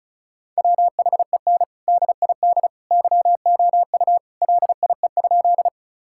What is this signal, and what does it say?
Morse code, heard as a keyed tone, about 34 wpm, 700 Hz. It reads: WHEN DID YOU LIE?
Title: Sentences from Top 1600 Words 35wpm